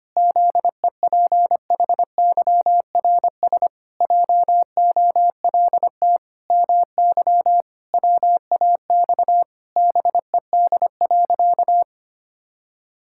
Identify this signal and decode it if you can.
Morse code, heard as a keyed tone, about 25 wpm, 700 Hz. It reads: ZEPHYRS JOLT MY WAX BED.
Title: Pangram Challenge v2 - 25wpm